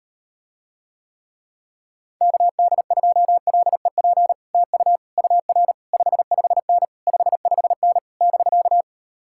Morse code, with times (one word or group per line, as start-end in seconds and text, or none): 2.2-4.3 KD2LEP
4.5-5.0 TU
5.2-5.7 UR
5.9-6.8 55N
7.1-8.0 55N
8.2-8.8 <BK>